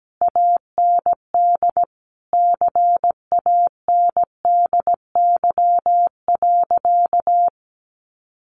Morse code, heard as a keyed tone, about 17 wpm, 700 Hz. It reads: AND CANDY.
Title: Easy Phrases 17wpm